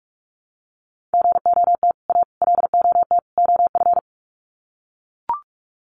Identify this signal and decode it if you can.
Morse code, heard as a keyed tone, about 44 wpm, 700 Hz. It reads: GOT A LOT OF
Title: Sets of 4 Words 45wpm